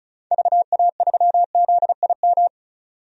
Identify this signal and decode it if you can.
Morse code, heard as a keyed tone, about 34 wpm, 700 Hz. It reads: VA3ZIM
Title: DXCC-100 Call Signs - 6 Characters - 35wpm